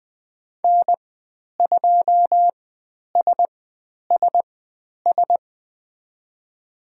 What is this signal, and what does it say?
Morse code, heard as a keyed tone, about 20 wpm, spaced as for 10 wpm, 700 Hz. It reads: N2SSS
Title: 102 Lesson 17g - Letter u - Call Signs 20wpm